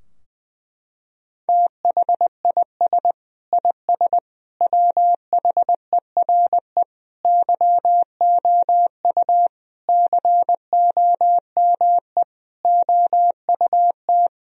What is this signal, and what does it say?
Morse code, heard as a keyed tone, about 21 wpm, 700 Hz. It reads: THIS IS WHERE YOU COME OUT
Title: Sentences from Top 1000 Words - Review Part 3 - 20wpm